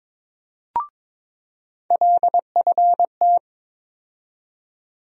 Morse code, heard as a keyed tone, about 22 wpm, 700 Hz. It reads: LFT
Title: USA Call Sign Postfixes - 3 Letters 22wpm